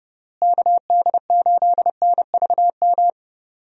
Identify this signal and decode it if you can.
Morse code, heard as a keyed tone, about 32 wpm, 700 Hz. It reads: KD8NVM
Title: USA Call Signs - 6 Characters - Repeated 2x - 30wpm